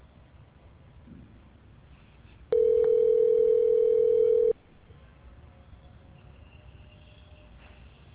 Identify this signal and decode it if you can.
Ringback tone (North America), 440 + 480 Hz.